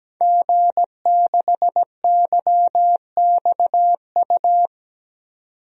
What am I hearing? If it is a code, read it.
Morse code, heard as a keyed tone, about 17 wpm, 700 Hz. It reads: G6YXU
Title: DXCC-100 Call Signs - 5 Characters - 17wpm